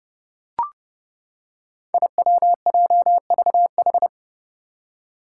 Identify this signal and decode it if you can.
Morse code, heard as a keyed tone, about 32 wpm, 700 Hz. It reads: IWJVH
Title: Five Challenging Letter-Number Combinations 30wpm